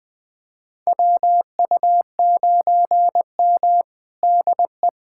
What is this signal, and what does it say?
Morse code, heard as a keyed tone, about 20 wpm, 700 Hz. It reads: WU9M DE
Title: USA Callsign DE Callsign - 4 Characters - 20wpm